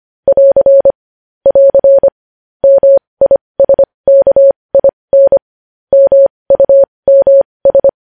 Morse code, heard as a keyed tone, about 25 wpm, spaced as for 20 wpm, 550 Hz. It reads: <AR> <AR> MISKIN MUMS
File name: podcast / News headlines-1x-25x20